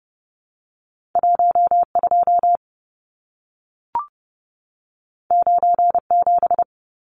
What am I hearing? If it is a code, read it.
Morse code, heard as a keyed tone, about 30 wpm, 700 Hz. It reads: 12 97
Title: Numbers - 2-Digit 30wpm